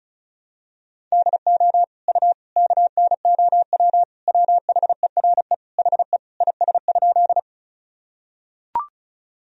Morse code, heard as a keyed tone, about 36 wpm, 700 Hz. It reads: DO U KNOW WHERE HE IS?
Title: Sentences from Top 300 Words and Abbreviations 35wpm